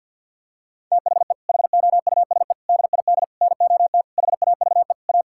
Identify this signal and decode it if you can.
Morse code, heard as a keyed tone, about 50 wpm, 700 Hz. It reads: THE HOUSE DID NOT HAVE A
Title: Sentences from Top 1600 Words 50wpm